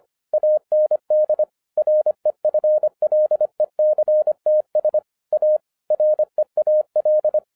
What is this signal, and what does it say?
Morse code, heard as a keyed tone, about 25 wpm, 610 Hz. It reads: AND REFLECTS A REAL
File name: 2012-01-mp3 / n14-FUN WITH NUMBERS-25wpm